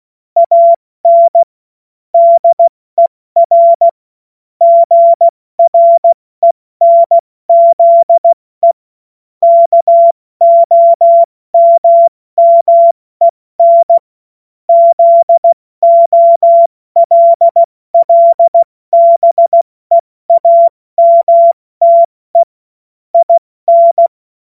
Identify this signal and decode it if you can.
Morse code, heard as a keyed tone, about 16 wpm, spaced as for 14 wpm, 680 Hz. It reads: AN DER GRENZE KOMMEN ZOLLBEAMTE IN
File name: Morsekurs / Witze / Witz 23. Mai 2023 70 F80